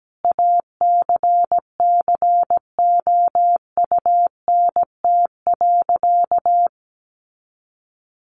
Morse code, heard as a keyed tone, about 17 wpm, 700 Hz. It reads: ACCOUNT.
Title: Sentences from Top 900 Words 17wpm